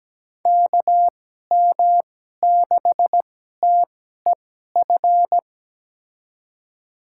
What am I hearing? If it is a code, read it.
Morse code, heard as a keyed tone, about 17 wpm, spaced as for 12 wpm, 700 Hz. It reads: KM6TEF
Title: USA Call Signs - 6 Characters - 2x Character Spacing - 17wpm